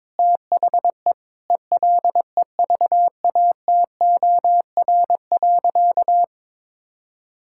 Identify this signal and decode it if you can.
Morse code, heard as a keyed tone, about 22 wpm, 700 Hz. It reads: THE ELEVATOR.
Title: Sentences from Top 1600 Words 22wpm